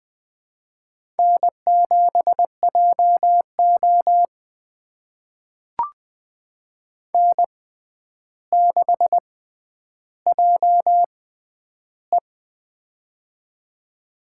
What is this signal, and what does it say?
Morse code, heard as a keyed tone, about 20 wpm, spaced as for 10 wpm, 700 Hz. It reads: N7JO N6JE
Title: USA Call Signs - 4 Characters - 6x Character Spacing - 20wpm